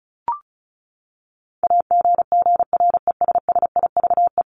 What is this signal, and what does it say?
Morse code, heard as a keyed tone, about 34 wpm, 700 Hz. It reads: AGGRESSIVE
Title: Top 1501-1750 Words 35wpm